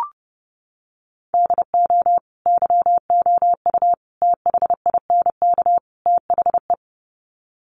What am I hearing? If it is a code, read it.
Morse code, heard as a keyed tone, about 30 wpm, 700 Hz. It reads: DO YOU THINK THE
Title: Sets of 4 Words 30wpm